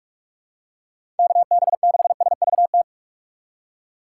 Morse code, heard as a keyed tone, about 46 wpm, 700 Hz. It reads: KB6SVT
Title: USA Call Signs - 6 Characters - 2x Character Spacing - 45wpm